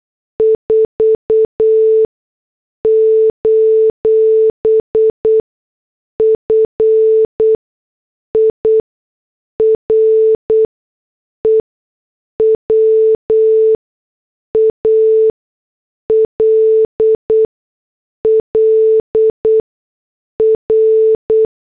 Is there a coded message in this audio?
Yes: Morse code from 0.4 to 21.4 s, heard as a keyed tone, about 8 wpm, spaced as for 6 wpm, 440 Hz. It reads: 4: FIREWALL R